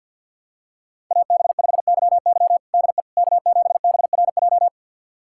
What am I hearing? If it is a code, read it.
Morse code, heard as a keyed tone, about 49 wpm, 700 Hz. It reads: AB5YY DE K7BRJ
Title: USA Callsign DE Callsign - 5 Characters - 50wpm